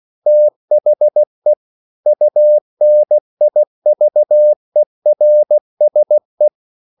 Morse code, heard as a keyed tone, about 16 wpm, 600 Hz. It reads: THE UNIVERSE
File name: CW / Practice / EnglishQuotations0416